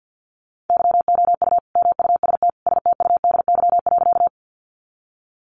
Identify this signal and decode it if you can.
Morse code, heard as a keyed tone, about 50 wpm, 700 Hz. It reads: YOU MUST STUDY.
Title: Sentences from Top 300 Words 50wpm